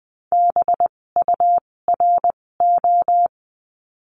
Morse code, heard as a keyed tone, about 20 wpm, spaced as for 16 wpm, 700 Hz. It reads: BURO
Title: 256 Lesson 42a - Words 12-20wpm